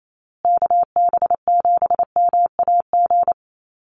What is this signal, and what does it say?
Morse code, heard as a keyed tone, about 28 wpm, 700 Hz. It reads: KB7MAG
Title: USA Call Signs Repeated 3x 28wpm